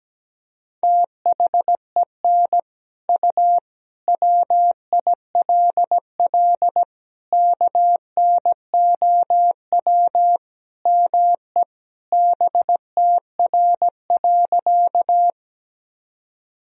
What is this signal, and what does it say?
Morse code, heard as a keyed tone, about 17 wpm, 700 Hz. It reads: THEN U WILL KNOW ME BTR.
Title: Sentences from Top 400 Words and Abbreviations 17wpm